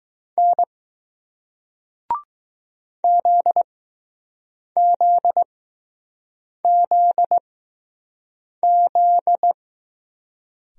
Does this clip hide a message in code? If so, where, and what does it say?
Yes: Morse code from 0.4 to 9.5 s, heard as a keyed tone, about 20 wpm, spaced as for 6 wpm, 700 Hz. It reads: N ZZZZ